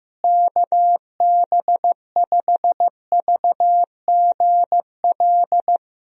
Morse code, heard as a keyed tone, about 15 wpm, 700 Hz. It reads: KB5VGL
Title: USA Call Signs - 6 Characters - Repeated 3x - 15wpm